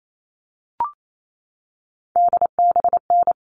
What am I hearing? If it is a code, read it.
Morse code, heard as a keyed tone, about 28 wpm, 700 Hz. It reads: DBN
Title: All Cut Numbers - 3-Digit 28wpm